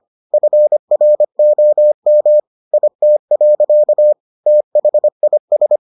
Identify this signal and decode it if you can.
Morse code, heard as a keyed tone, about 25 wpm, 600 Hz. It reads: FROM IT. THIS